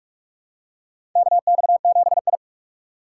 Morse code, heard as a keyed tone, about 45 wpm, 700 Hz. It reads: KX7I
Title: USA Call Signs - 4 Characters - 3x Character Spacing - 45wpm